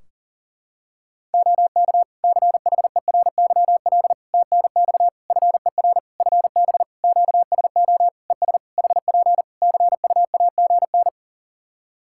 Morse code, heard as a keyed tone, about 37 wpm, 700 Hz. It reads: OK CHERYL TNX FER FB QSO ES HP CUAGN
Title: Practice QSO - Exchange 4 - Ending - 40wpm